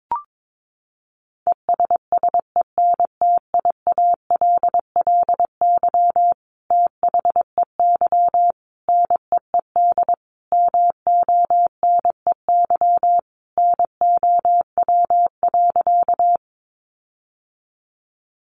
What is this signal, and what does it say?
Morse code, heard as a keyed tone, about 22 wpm, 700 Hz. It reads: ESSENTIALLY THEY NEED MONEY NOW.
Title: Sentences from Top 1300 Words 22wpm